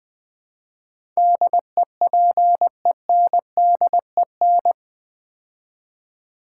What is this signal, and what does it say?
Morse code, heard as a keyed tone, about 20 wpm, 700 Hz. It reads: DEPENDEN
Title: Build A Word - 4 Letters At A Time - 10-Letter Words - 20wpm